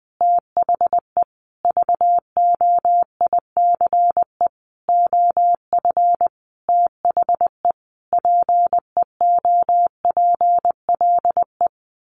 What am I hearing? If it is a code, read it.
Morse code, heard as a keyed tone, about 20 wpm, 700 Hz. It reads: THE VOICE OF THE PEOPLE
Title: Sentences from Top 1500 Words 20wpm